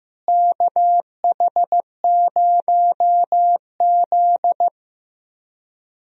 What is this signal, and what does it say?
Morse code, heard as a keyed tone, about 15 wpm, 700 Hz. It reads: KH0Z
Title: USA Call Signs - 4 Characters - Repeated 3x - 15wpm